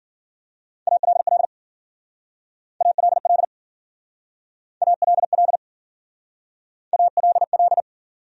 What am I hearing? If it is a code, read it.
Morse code, heard as a keyed tone, about 52 wpm, 700 Hz. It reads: ALL ALL ALL ALL